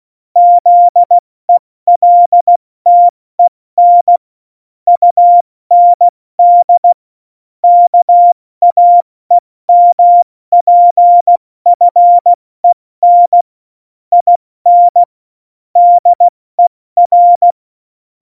Morse code, heard as a keyed tone, about 16 wpm, 700 Hz. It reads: ZELTEN UND KAEMPFEN IN DER
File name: Morsekurs / Witze / Witz 25. September 2018, 70 F80